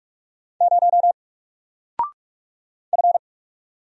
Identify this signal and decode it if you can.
Morse code, heard as a keyed tone, about 44 wpm, 700 Hz. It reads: 0 F
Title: Single Letter-Number Alt 45wpm